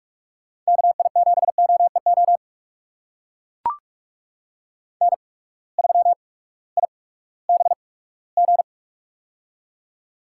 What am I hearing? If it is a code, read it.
Morse code, heard as a keyed tone, about 44 wpm, spaced as for 20 wpm, 700 Hz. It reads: KI7OEO N3IBG
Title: USA Call Signs - 8x Character Spacing - 45wpm